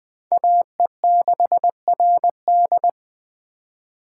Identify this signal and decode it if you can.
Morse code, heard as a keyed tone, about 20 wpm, 700 Hz. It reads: AE6RD
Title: USA Call Signs - 5 Characters - 2x Character Spacing - 20wpm